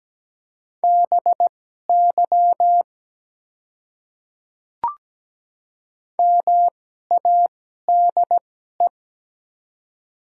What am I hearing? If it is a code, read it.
Morse code, heard as a keyed tone, about 17 wpm, spaced as for 12 wpm, 700 Hz. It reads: BY MADE
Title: 2x Character Spacing - Top 200 Words - 17wpm